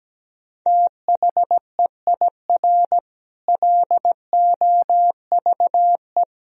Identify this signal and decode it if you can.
Morse code, heard as a keyed tone, about 17 wpm, 700 Hz. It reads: THEIR LOVE
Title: Sentences from Top 1000 Words - Review Part 4 - 17wpm